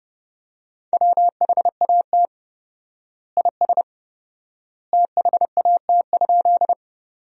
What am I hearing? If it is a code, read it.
Morse code, heard as a keyed tone, about 32 wpm, 700 Hz. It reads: WHAT IS THAT?
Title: 4x Word-Spacing - Sentences from Top 100 Words - 30wpm